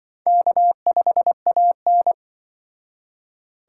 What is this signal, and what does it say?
Morse code, heard as a keyed tone, about 26 wpm, 700 Hz. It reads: K5AN